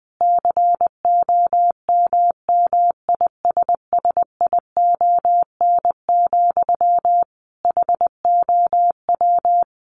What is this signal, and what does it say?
Morse code, heard as a keyed tone, about 19 wpm, 700 Hz. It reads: COMMISSION, HOW
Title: Sentences from Top 1400 Words 20wpm